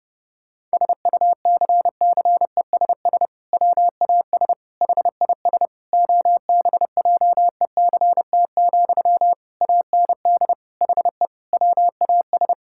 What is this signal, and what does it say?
Morse code, heard as a keyed tone, about 29 wpm, 700 Hz. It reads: SUCCESS WAS HIS OBJECT, AND HE WAS
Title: Sentences from Top 1000 Words - Review Part 6 - 30wpm